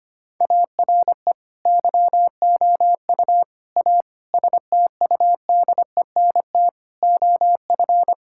Morse code, heard as a keyed tone, about 25 wpm, 700 Hz. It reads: ARE YOU A STUDENT OF
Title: Sentences from Top 1000 Words - Review Part 6 - 25wpm